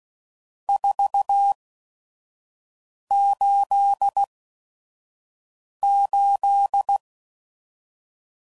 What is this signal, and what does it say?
Morse code, heard as a keyed tone, about 16 wpm, spaced as for 5 wpm, 800 Hz. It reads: 488
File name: CW Code Groups @5WPM / RST's / Various RST No's